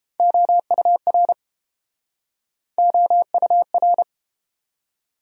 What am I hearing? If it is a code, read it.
Morse code, heard as a keyed tone, about 29 wpm, 700 Hz. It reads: OUR OUR